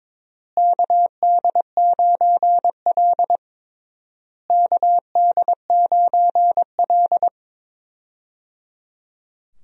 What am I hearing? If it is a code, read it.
Morse code, heard as a keyed tone, about 22 wpm, 700 Hz. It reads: KD9L KD9L
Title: USA Call Signs - 4 Characters - Repeated 3x - 22wpm